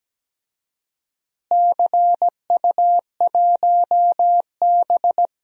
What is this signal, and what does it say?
Morse code, heard as a keyed tone, about 17 wpm, 700 Hz. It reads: CU1B
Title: DXCC-100 Call Signs - 4 Characters - Repeated 3x - 17wpm